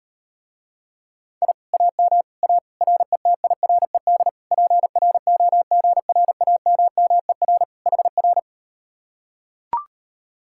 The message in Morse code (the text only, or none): I AM A RETIRED PROGRAMMER HR